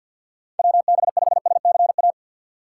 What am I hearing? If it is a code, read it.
Morse code, heard as a keyed tone, about 52 wpm, 700 Hz. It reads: WB5SCA
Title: USA Call Signs - 4x Character Spacing - 50wpm